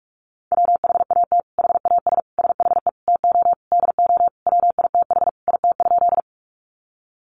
Morse code, heard as a keyed tone, about 44 wpm, 700 Hz. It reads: WHAT HAS SHE TO DO WITH IT?
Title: Sentences from Top 100 Words 45wpm